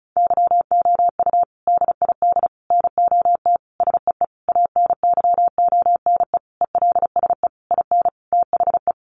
Morse code, heard as a keyed tone, about 34 wpm, 700 Hz. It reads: YOU DID NOT SEE ANYONE ELSE IN THE
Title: Sentences from Top 900 Words 35wpm